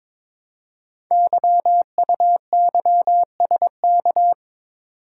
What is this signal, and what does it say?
Morse code, heard as a keyed tone, about 22 wpm, 700 Hz. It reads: YUYSK
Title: Code Groups - 5 Letters - 22wpm